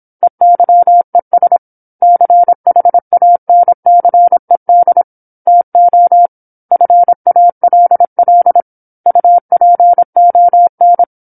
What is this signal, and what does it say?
Morse code, heard as a keyed tone, about 27 wpm, 700 Hz. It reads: EYES CHANCED TO FALL UPON
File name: 08-Chapter-08 26 WPM